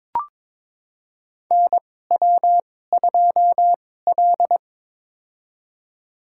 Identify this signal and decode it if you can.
Morse code, heard as a keyed tone, about 22 wpm, spaced as for 16 wpm, 700 Hz. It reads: NW2L
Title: USA Call Signs - 2x Character Spacing - 22wpm